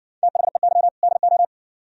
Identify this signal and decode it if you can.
Morse code, heard as a keyed tone, about 60 wpm, 700 Hz. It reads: THEY DO